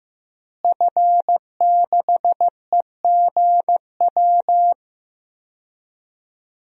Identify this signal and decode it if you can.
Morse code, heard as a keyed tone, about 15 wpm, 700 Hz. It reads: F6EGW
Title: DXCC-100 Call Signs - 5 Characters - Repeated 3x - 15wpm